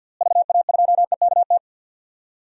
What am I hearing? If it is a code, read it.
Morse code, heard as a keyed tone, about 49 wpm, 700 Hz. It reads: VA2EKT
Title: DXCC-100 Call Signs - 6 Characters - Repeated 2x - 50wpm